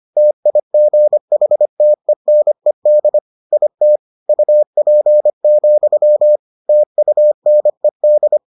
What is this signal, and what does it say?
Morse code, heard as a keyed tone, about 25 wpm, 600 Hz. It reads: TIGHTENED IT UP, TUNED